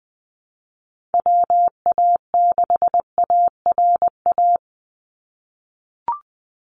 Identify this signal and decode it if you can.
Morse code, heard as a keyed tone, about 19 wpm, 700 Hz. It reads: WA6ARA
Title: NA SOTA Chaser Callsigns 20wpm